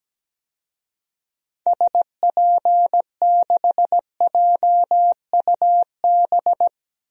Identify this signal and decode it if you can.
Morse code, heard as a keyed tone, about 17 wpm, 700 Hz. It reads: SP6JUB
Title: DXCC-100 Call Signs - 6 Characters - 17wpm